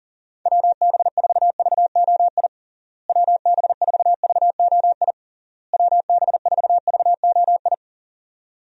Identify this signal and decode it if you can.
Morse code, heard as a keyed tone, about 37 wpm, 700 Hz. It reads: WB4VOI WB4VOI WB4VOI
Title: USA Call Signs - 6 Characters - Repeated 3x - 40wpm